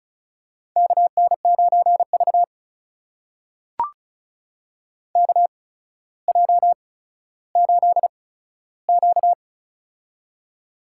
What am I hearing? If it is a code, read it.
Morse code, heard as a keyed tone, about 34 wpm, spaced as for 15 wpm, 700 Hz. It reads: KN9V KJ8Q
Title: USA Call Signs - 4 Characters - 8x Character Spacing - 35wpm